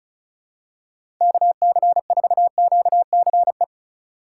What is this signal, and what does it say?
Morse code, heard as a keyed tone, about 34 wpm, 700 Hz. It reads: KC4QCE
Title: USA Call Signs - 6 Characters - 3x Character Spacing - 35wpm